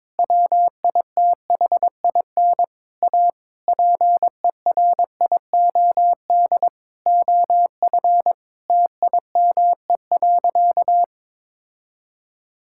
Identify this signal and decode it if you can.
Morse code, heard as a keyed tone, about 22 wpm, 700 Hz. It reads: WITHIN A PERIOD OF TIME.
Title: Sentences from Top 1000 Words - Review Part 5 - 22wpm